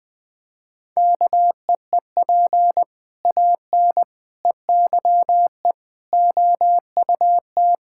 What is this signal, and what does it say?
Morse code, heard as a keyed tone, about 20 wpm, 700 Hz. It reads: KEEP AN EYE OUT
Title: Sentences with 7-Word Phrases - 20wpm